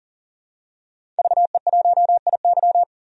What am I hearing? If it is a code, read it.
Morse code, heard as a keyed tone, about 40 wpm, 700 Hz. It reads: VE1IY
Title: DXCC-100 Call Signs - 5 Characters - 40wpm